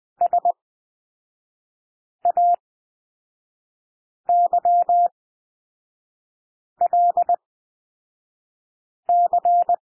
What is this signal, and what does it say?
Morse code, heard as a keyed tone, about 20 wpm, spaced as for 5 wpm, 700 Hz. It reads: SAYLC
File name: SOUND LETTERS 20 wpm